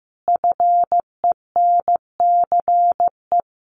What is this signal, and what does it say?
Morse code, heard as a keyed tone, about 15 wpm, 700 Hz. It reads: FENCE
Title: A Sound Of Thunder 15wpm